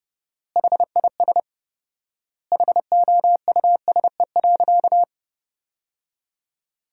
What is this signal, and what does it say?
Morse code, heard as a keyed tone, about 32 wpm, 700 Hz. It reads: HIS HOUSE.
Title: 4x Word-Spacing - Sentences from Top 300 Words - 30wpm